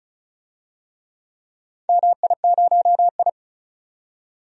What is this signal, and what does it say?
Morse code, heard as a keyed tone, about 36 wpm, 700 Hz. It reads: MI0I
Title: DXCC-100 Call Signs - 4 Characters - 35wpm